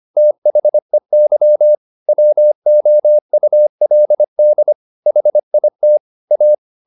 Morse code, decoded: THEY WOULD HIT A

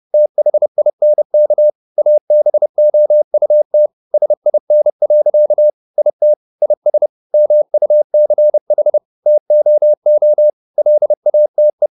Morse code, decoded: THINK ABOUT SIN. IT IS MUCH TOO LATE

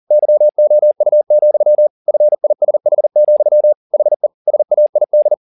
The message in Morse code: YOU, FISH, HE SAID

